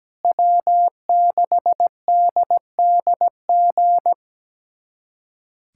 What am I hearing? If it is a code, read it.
Morse code, heard as a keyed tone, about 17 wpm, 700 Hz. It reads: W6DDG
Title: USA Callsign DE Callsign 17wpm